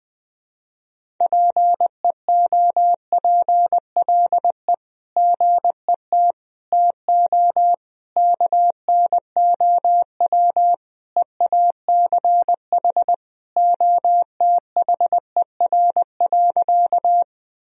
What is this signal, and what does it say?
Morse code, heard as a keyed tone, about 21 wpm, 700 Hz. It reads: PEOPLE GET TO KNOW EACH OTHER.
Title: Sentences from Top 1000 Words - Review Part 1 - 20wpm